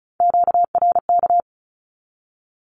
Morse code, heard as a keyed tone, about 34 wpm, 700 Hz. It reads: QRK
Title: Q-Codes 35wpm